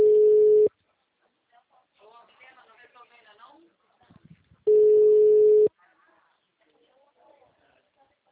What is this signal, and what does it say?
Ringback tone (Europe, 425 Hz), with speech.